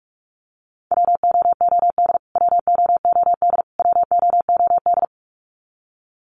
Morse code, heard as a keyed tone, about 44 wpm, 700 Hz. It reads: WOOD WOOD WOOD
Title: Top 300 Words Repeated 45wpm